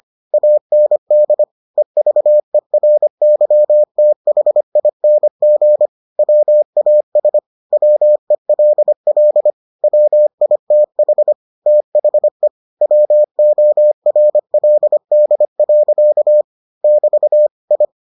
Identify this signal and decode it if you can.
Morse code, heard as a keyed tone, about 25 wpm, 600 Hz. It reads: AND EVERYTHING WAS WELL WITH THE WORLD. <BT> I